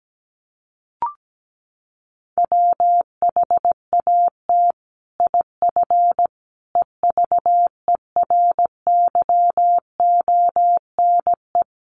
Morse code, heard as a keyed tone, about 17 wpm, 700 Hz. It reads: WHAT IF EVERYONE
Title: Sentences from Top 1400 Words 17wpm